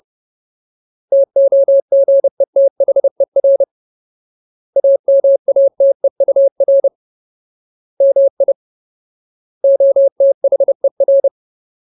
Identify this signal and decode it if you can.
Morse code, heard as a keyed tone, about 30 wpm, 550 Hz. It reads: TOGETHER AMATEUR MI OTHER
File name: Ultimate Word List / 30WPM Ultimate